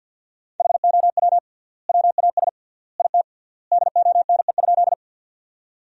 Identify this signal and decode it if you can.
Morse code, heard as a keyed tone, about 48 wpm, 700 Hz. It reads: HOW WAS IT DONE?
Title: 3x Word-Spacing - Sentences from Top 500 Words - 50wpm